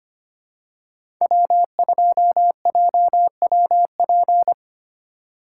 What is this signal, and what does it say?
Morse code, heard as a keyed tone, about 25 wpm, 700 Hz. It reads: W2JWP
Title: USA Call Signs - 4x Character Spacing - 25wpm